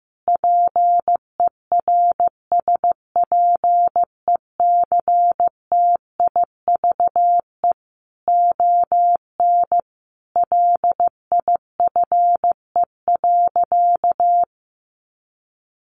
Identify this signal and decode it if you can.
Morse code, heard as a keyed tone, about 15 wpm, 700 Hz. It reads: PERSPECTIVE ON LIFE.
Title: Sentences from Top 1200 Words 15wpm